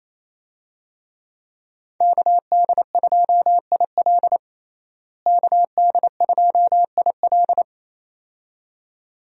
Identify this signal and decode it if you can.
Morse code, heard as a keyed tone, about 28 wpm, 700 Hz. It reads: KD2IL KD2IL
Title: USA Call Signs - 5 Characters - Repeated 2x - 28wpm